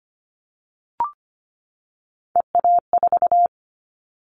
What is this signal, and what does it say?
Morse code, heard as a keyed tone, about 25 wpm, 700 Hz. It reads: EA4
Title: Callsign Prefixes for Top 25 DXCC - 25wpm